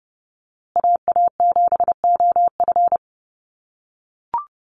Morse code, heard as a keyed tone, about 28 wpm, 700 Hz. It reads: AA7OF
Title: USA Call Signs Repeated 3x 30wpm